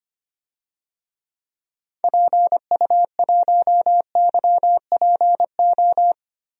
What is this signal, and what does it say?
Morse code, heard as a keyed tone, about 25 wpm, 700 Hz. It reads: PU1YPO
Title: DXCC-100 Call Signs - 6 Characters - 25wpm